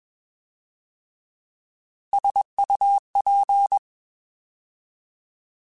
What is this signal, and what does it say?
Morse code, heard as a keyed tone, about 21 wpm, 800 Hz. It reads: SUP
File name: Alphabetical / Alphabetical mp3s / Ss @20